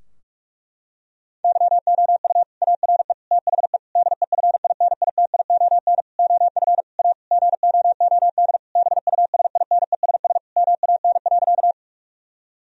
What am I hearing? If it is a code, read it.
Morse code, heard as a keyed tone, about 46 wpm, 700 Hz. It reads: YOU ARE THE DEFINITION OF A GOOD BUSINESS MAN.
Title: Sentences from Top 1000 Words - Review Part 4 - 45wpm